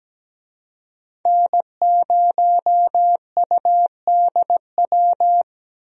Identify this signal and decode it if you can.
Morse code, heard as a keyed tone, about 17 wpm, 700 Hz. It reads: N0UDW
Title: USA Call Signs - 5 Characters - 17wpm